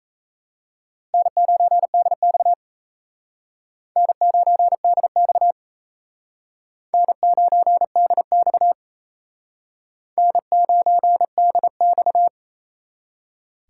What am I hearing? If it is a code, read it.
Morse code, heard as a keyed tone, about 35 wpm, 700 Hz. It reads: N9DX N9DX N9DX N9DX